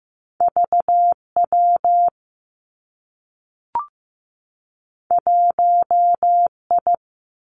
Morse code, heard as a keyed tone, about 15 wpm, 700 Hz. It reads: V W 1 I
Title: Two Challenging Letter-Number Combinations 15wpm